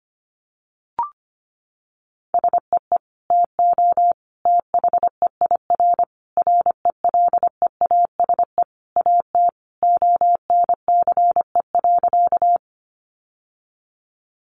Morse code, heard as a keyed tone, about 25 wpm, 700 Hz. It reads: SEE TO THEIR RELEASE AT ONCE.
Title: Sentences from Top 1100 Words 25wpm